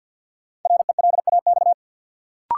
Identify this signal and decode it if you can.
Morse code, heard as a keyed tone, about 52 wpm, 700 Hz. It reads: RELAX
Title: Sentences from Top 1400 Words - New Word Drill - 50wpm